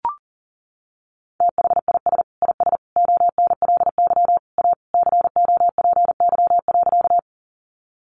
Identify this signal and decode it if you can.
Morse code, heard as a keyed tone, about 40 wpm, 700 Hz. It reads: THIS IS ONLY A COPY.
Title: Sentences from Top 1400 Words 40wpm